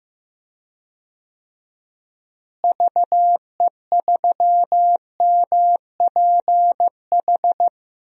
Morse code, heard as a keyed tone, about 15 wpm, 700 Hz. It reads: VE3MPH
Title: DXCC-100 Call Signs - 6 Characters - Repeated 3x - 15wpm